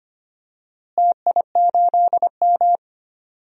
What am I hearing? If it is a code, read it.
Morse code, heard as a keyed tone, about 25 wpm, 700 Hz. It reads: TI8M
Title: DXCC-100 Call Signs - 4 Characters - Repeated 2x - 25wpm